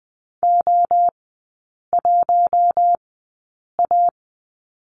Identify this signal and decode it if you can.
Morse code, heard as a keyed tone, about 20 wpm, spaced as for 8 wpm, 700 Hz. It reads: O1A
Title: 198 Lesson 32e - Letter k - Three Characters 20wpm